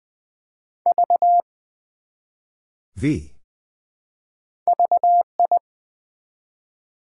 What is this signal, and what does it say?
Morse code, heard as a keyed tone, about 21 wpm, 700 Hz. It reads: V VI
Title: Build A Word - 1 Letter At A Time - 4-Letter Words - 20wpm